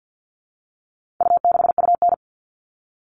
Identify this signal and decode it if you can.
Morse code, heard as a keyed tone, about 50 wpm, 700 Hz. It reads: U6UN
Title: All Cut Numbers - 4-Digit 50wpm